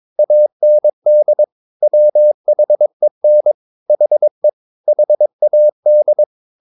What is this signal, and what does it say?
Morse code, heard as a keyed tone, about 22 wpm, 600 Hz. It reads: AND WHEN HE HAD
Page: Romanen Never af Ken Follet på engelsk.